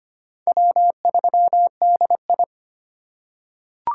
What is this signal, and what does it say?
Morse code, heard as a keyed tone, about 25 wpm, 700 Hz. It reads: W3DI
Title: USA Call Signs - 4 Characters - 3x Character Spacing - 25wpm